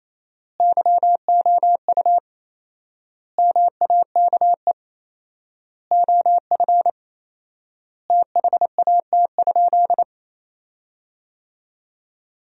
Morse code, heard as a keyed tone, about 28 wpm, 700 Hz. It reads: YOU MAKE OF THAT?
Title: 4x Word-Spacing - Sentences from Top 100 Words - 28wpm